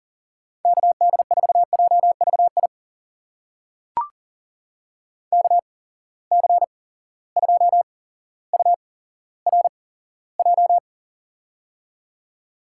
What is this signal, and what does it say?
Morse code, heard as a keyed tone, about 40 wpm, spaced as for 17 wpm, 700 Hz. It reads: KD4JVI KC2URJ
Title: USA Call Signs - 6 Characters - 8x Character Spacing - 40wpm